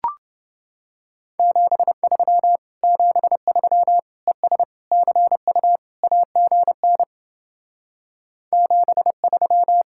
Morse code, heard as a keyed tone, about 29 wpm, 700 Hz. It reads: 73 73 ES CU AGN 73